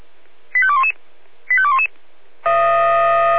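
Five-tone selective call (ZVEI), heard as quick runs of five tones, tones of about 70 ms. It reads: 85211 85211
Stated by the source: ZVEI five-tone selective call 85211, received on an SDR.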